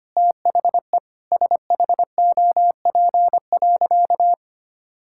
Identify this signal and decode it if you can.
Morse code, heard as a keyed tone, about 25 wpm, 700 Hz. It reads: THE SHOP.
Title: Sentences from Top 1600 Words 25wpm